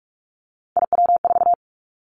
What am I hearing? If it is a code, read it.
Morse code, heard as a keyed tone, about 46 wpm, 700 Hz. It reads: IW4
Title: Callsign Prefixes for Top 50 DXCC - 45wpm